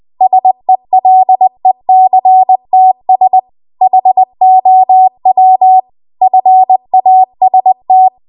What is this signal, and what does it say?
Morse code, heard as a keyed tone, about 20 wpm, 750 Hz. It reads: SELECTS HOW FAST
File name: Archive / 20 WPM / 20WPM